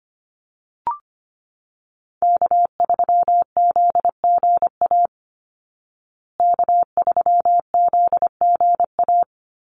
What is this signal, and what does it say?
Morse code, heard as a keyed tone, about 25 wpm, 700 Hz. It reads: K3ZGA K3ZGA
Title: Call Signs Repeated 3x - 25wpm